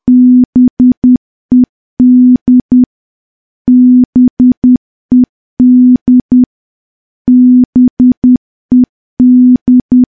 Morse code, heard as a keyed tone, about 10 wpm, 250 Hz. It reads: BED BED BED
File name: morse / morse-training-10WPM